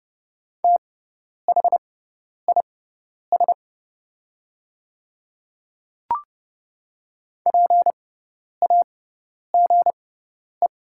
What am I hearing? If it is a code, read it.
Morse code, heard as a keyed tone, about 30 wpm, spaced as for 10 wpm, 700 Hz. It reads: THIS PAGE
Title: 6x Character Spacing - Top 300 Words - 30wpm